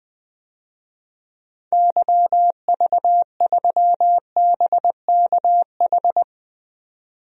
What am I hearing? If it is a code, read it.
Morse code, heard as a keyed tone, about 20 wpm, 700 Hz. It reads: YV3BKH
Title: DXCC-100 Call Signs - 6 Characters - 20wpm